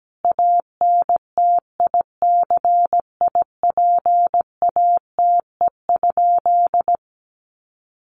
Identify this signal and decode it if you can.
Morse code, heard as a keyed tone, about 17 wpm, 700 Hz. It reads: ANTICIPATE?
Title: Sentences from Top 1500 Words 17wpm